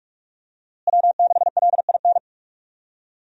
Morse code, heard as a keyed tone, about 46 wpm, 700 Hz. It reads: W6LIN
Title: USA Call Signs - 5 Characters - Repeated 2x - 45wpm